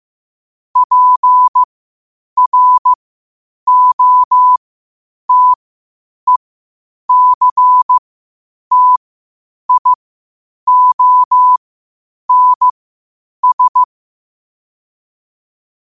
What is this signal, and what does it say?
Morse code, heard as a keyed tone, about 15 wpm, spaced as for 8 wpm, 1000 Hz. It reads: PROTECTIONS
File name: qotdfiles / 8wpm